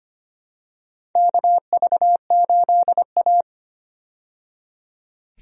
Morse code, heard as a keyed tone, about 25 wpm, 700 Hz. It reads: KV8A
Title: USA Call Signs - 4 Characters - 25wpm